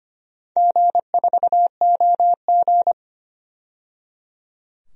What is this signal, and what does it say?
Morse code, heard as a keyed tone, about 25 wpm, 700 Hz. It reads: G4OG
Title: DXCC-100 Call Signs - 4 Characters - 25wpm